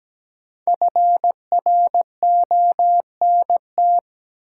Morse code, heard as a keyed tone, about 17 wpm, 700 Hz. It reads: FRONT
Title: Sentences from Top 500 Words - New Word Drill - 17wpm